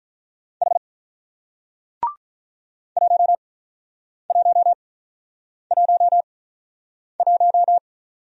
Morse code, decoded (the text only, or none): H 1111